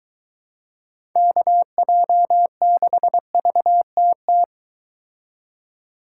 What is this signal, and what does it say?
Morse code, heard as a keyed tone, about 22 wpm, 700 Hz. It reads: KJ6VTT